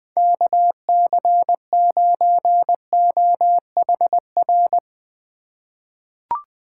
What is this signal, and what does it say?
Morse code, heard as a keyed tone, about 21 wpm, 700 Hz. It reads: KC9OHR
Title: USA Call Signs - 6 Characters - 3x Character Spacing - 20wpm